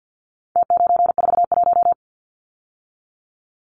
Morse code, heard as a keyed tone, about 50 wpm, 700 Hz. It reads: T941
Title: Numbers - Contest Style - 4-Digit 50wpm